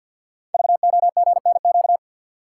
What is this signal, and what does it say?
Morse code, heard as a keyed tone, about 50 wpm, 690 Hz. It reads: VOGNQ
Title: Code Groups - 5 Letters - 50wpm